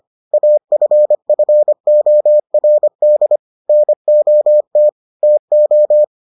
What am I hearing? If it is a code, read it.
Morse code, heard as a keyed tone, about 25 wpm, 600 Hz. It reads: AFFORD NOT TO